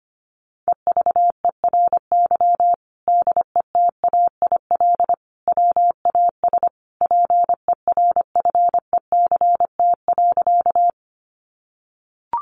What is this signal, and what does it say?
Morse code, heard as a keyed tone, about 25 wpm, 700 Hz. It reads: EVERY DETAIL WAS PERFECT.
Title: Sentences from Top 1500 Words 25wpm